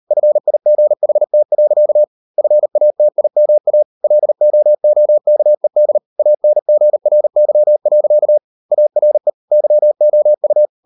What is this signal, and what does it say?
Morse code, heard as a keyed tone, about 38 wpm, 600 Hz. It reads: FIGHT. FATIMA LOOKED ANGRY. ARE YOU